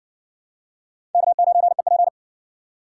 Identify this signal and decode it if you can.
Morse code, heard as a keyed tone, about 54 wpm, 700 Hz. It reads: K9EP